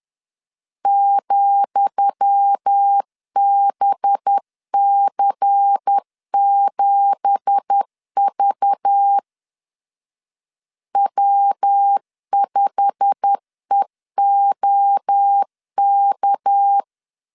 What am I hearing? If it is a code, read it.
Morse code, heard as a keyed tone, about 10 wpm, 800 Hz. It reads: ,BC7V W5EOK
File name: LEZIONI / 10 SIGNS / 10D COMA 10wpm 3013